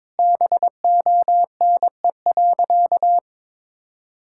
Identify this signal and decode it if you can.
Morse code, heard as a keyed tone, about 22 wpm, 700 Hz. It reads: BONE.
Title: Sentences from Top 1600 Words 22wpm